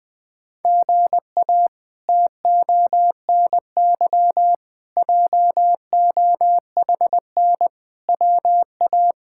Morse code, heard as a keyed tone, about 21 wpm, 700 Hz. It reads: GA TONY JOHN WA
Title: K1USN SST QSO Practice - 20wpm